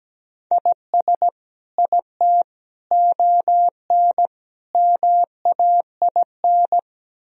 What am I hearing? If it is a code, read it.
Morse code, heard as a keyed tone, about 17 wpm, 700 Hz. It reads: IS IT ON MAIN
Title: Sentences from Top 1000 Words - Review Part 2 - 17wpm